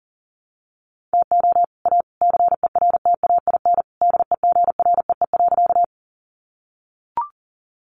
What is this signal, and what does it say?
Morse code, heard as a keyed tone, about 38 wpm, 700 Hz. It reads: TO A CERTAIN DEGREE.
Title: Sentences from Top 1500 Words 40wpm